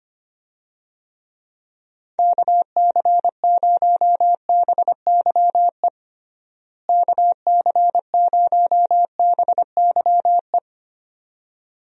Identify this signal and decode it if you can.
Morse code, heard as a keyed tone, about 25 wpm, 700 Hz. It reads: KC0BYE KC0BYE
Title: USA Call Signs - 6 Characters - Repeated 2x - 25wpm